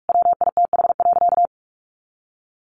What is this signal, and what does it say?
Morse code, heard as a keyed tone, about 44 wpm, 700 Hz. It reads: WITH.
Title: Sentences from Top 1500 Words 45wpm